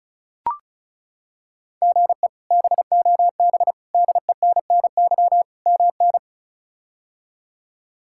Morse code, heard as a keyed tone, about 36 wpm, 700 Hz. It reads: GE BOB DENNY MN
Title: K1USN SST QSO Practice - 35wpm